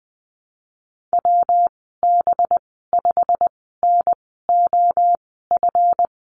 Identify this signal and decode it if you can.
Morse code, heard as a keyed tone, about 19 wpm, spaced as for 14 wpm, 700 Hz. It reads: WB5NOF
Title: 186 Lesson 30g - Letter b - Call Signs 20wpm